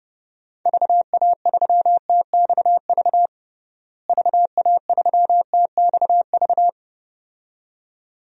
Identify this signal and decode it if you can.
Morse code, heard as a keyed tone, about 30 wpm, 700 Hz. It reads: VA3TXV VA3TXV
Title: DXCC-100 Call Signs - 6 Characters - Repeated 2x - 30wpm